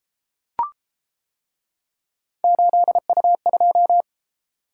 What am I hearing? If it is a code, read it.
Morse code, heard as a keyed tone, about 31 wpm, 700 Hz. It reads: 8U2